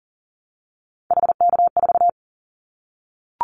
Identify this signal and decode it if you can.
Morse code, heard as a keyed tone, about 43 wpm, 700 Hz. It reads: HK4
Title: Callsign Prefixes for Top 100 DXCC - 40wpm